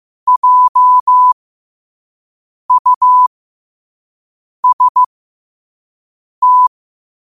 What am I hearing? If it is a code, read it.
Morse code, heard as a keyed tone, about 15 wpm, spaced as for 5 wpm, 1000 Hz. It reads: JUST
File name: qotdfiles / 5wpm